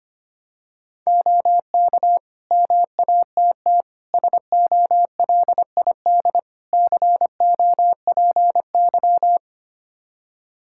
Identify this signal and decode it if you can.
Morse code, heard as a keyed tone, about 25 wpm, 700 Hz. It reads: OK MATT SOLID COPY
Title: Practice QSO - Exchange 3 - More Info - 25wpm